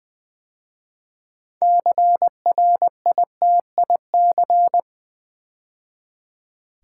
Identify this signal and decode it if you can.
Morse code, heard as a keyed tone, about 20 wpm, 700 Hz. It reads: CRITIC
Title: 6 Letter Words - First 3 Letters Given - 20wpm